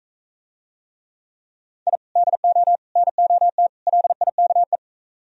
Morse code, heard as a keyed tone, about 45 wpm, 700 Hz. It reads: I DO NOT LIKE